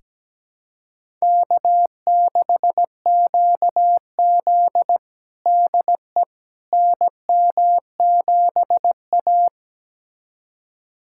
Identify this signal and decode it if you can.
Morse code, heard as a keyed tone, about 17 wpm, 700 Hz. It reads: K6QZ DE NM7A
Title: USA Callsign DE Callsign - 4 Characters - 17wpm